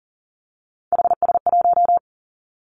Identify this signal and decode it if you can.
Morse code, heard as a keyed tone, about 42 wpm, 700 Hz. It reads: HS1
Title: Three Challenging Letter-Number Combinations 40wpm